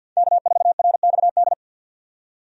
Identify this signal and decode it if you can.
Morse code, heard as a keyed tone, about 46 wpm, 700 Hz. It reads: K4RXD
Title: USA Call Signs - 5 Characters - 50wpm